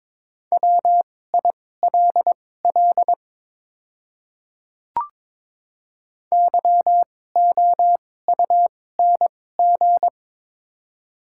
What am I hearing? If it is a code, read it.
Morse code, heard as a keyed tone, about 22 wpm, spaced as for 16 wpm, 700 Hz. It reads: WILL YOUNG
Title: 2x Character Spacing - Top 500 Words - 22wpm